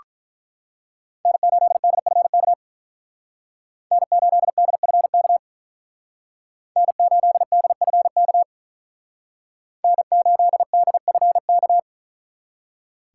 Morse code, heard as a keyed tone, about 45 wpm, 700 Hz. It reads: N8DFK N8DFK N8DFK N8DFK